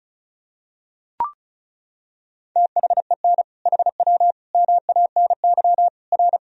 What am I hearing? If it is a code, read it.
Morse code, heard as a keyed tone, about 34 wpm, 700 Hz. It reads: THEN HW MANY R
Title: Sentences from Top 100 Words and Abbreviations 35wpm